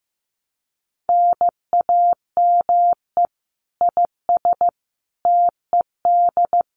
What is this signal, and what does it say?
Morse code, heard as a keyed tone, about 15 wpm, 700 Hz. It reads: NAME IS TED
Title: Common QSO Phrases 15wpm